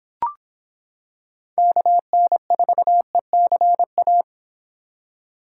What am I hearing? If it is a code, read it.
Morse code, heard as a keyed tone, about 25 wpm, 700 Hz. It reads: KN4ECA